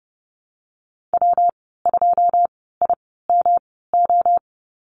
Morse code, heard as a keyed tone, about 31 wpm, spaced as for 17 wpm, 700 Hz. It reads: W2IMO
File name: Lesson 19g - Letter w - Call Signs 30wpm